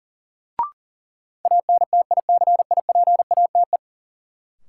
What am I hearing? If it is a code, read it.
Morse code, heard as a keyed tone, about 40 wpm, 700 Hz. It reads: ANTICIPATE
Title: Sentences from Top 1500 Words - New Word Drill Alt - 40wpm